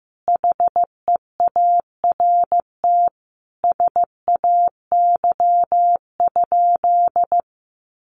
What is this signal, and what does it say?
Morse code, heard as a keyed tone, about 15 wpm, 700 Hz. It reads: HEART SAY?
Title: Sentences from Top 1200 Words 15wpm